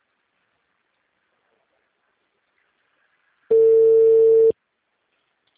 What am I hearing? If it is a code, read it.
Ringback tone (China, 450 Hz).